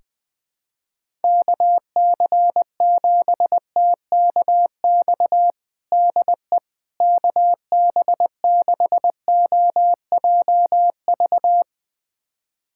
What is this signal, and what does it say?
Morse code, heard as a keyed tone, about 20 wpm, 700 Hz. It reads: KC7TKX DE KB6OJV
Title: USA Callsign DE Callsign - 6 Characters - 20wpm